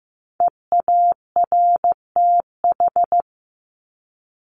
Morse code, heard as a keyed tone, about 15 wpm, 700 Hz. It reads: EARTH
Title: Top 300 Words Repeated 15wpm